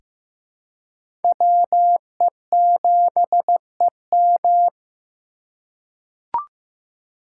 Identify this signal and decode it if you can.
Morse code, heard as a keyed tone, about 15 wpm, 700 Hz. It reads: WE7EM
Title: USA Call Signs - 5 Characters - 2x Character Spacing - 15wpm